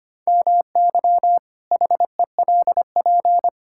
Morse code, heard as a keyed tone, about 25 wpm, 700 Hz. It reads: MY HELP
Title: Pangram Challenge v2 - 25wpm